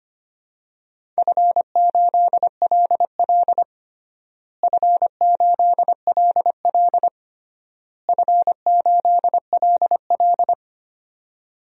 Morse code, heard as a keyed tone, about 25 wpm, 700 Hz. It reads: F8LL F8LL F8LL
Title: DXCC-100 Call Signs - 4 Characters - Repeated 3x - 25wpm